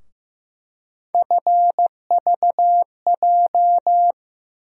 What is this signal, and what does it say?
Morse code, heard as a keyed tone, about 15 wpm, 700 Hz. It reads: FVJ
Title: USA Call Sign Postfixes - 3 Letters 15wpm